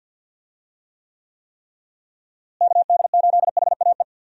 Morse code, heard as a keyed tone, about 46 wpm, 700 Hz. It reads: KD8HAE
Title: USA Call Signs - 6 Characters - Repeated 3x - 50wpm